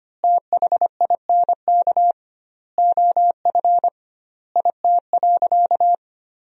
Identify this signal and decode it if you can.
Morse code, heard as a keyed tone, about 25 wpm, 700 Hz. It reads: THINK OF IT.
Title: 2x Word-Spacing - Sentences from Top 200 Words - 25wpm